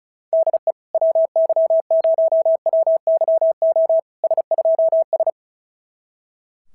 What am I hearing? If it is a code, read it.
Morse code, heard as a keyed tone, about 34 wpm, 650 Hz. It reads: DE WY0WYO S2S
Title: SOTA QSO Practice 35wpm